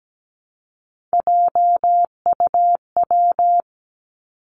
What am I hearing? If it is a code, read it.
Morse code, heard as a keyed tone, about 17 wpm, 700 Hz. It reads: JUW
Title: Three Challenging Letter-Number Combinations 17wpm